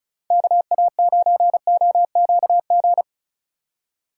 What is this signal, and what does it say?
Morse code, heard as a keyed tone, about 35 wpm, 700 Hz. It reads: KA9OQG
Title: Speed Racing - USA 2x3 Call Signs 35wpm